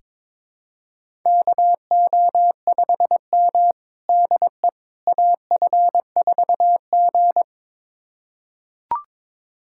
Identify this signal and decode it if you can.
Morse code, heard as a keyed tone, about 21 wpm, 700 Hz. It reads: KO5M DE AF4G
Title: USA Callsign DE Callsign - 4 Characters - 22wpm